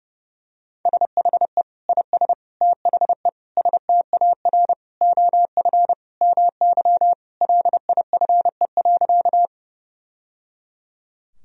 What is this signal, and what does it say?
Morse code, heard as a keyed tone, about 30 wpm, 700 Hz. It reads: SHE IS THE STAR OF MY LIFE.
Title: Sentences from Top 1000 Words - Review Part 2 - 30wpm